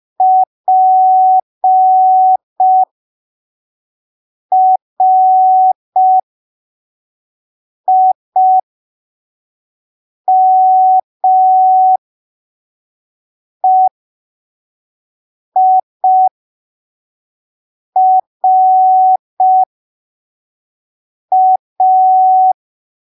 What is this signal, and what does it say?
Morse code, heard as a keyed tone, about 5 wpm, spaced as for 3 wpm, 750 Hz. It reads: PRIMEIRA
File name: arquivos / texto / texto 14 05ppm